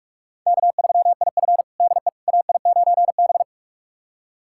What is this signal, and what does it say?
Morse code, heard as a keyed tone, about 46 wpm, 700 Hz. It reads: K3IF DE AI9B
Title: USA Callsign DE Callsign - 4 Characters - 45wpm